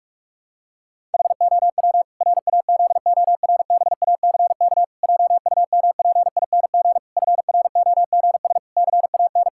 Morse code, heard as a keyed tone, about 48 wpm, 700 Hz. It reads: HOW RAZORBACK JUMPING FROGS CAN
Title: Pangram Challenge v2 - 45wpm